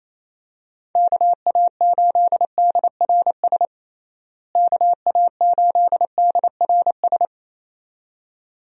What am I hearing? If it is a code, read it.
Morse code, heard as a keyed tone, about 28 wpm, 700 Hz. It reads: KA8DRS KA8DRS
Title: USA Call Signs - 6 Characters - Repeated 2x - 28wpm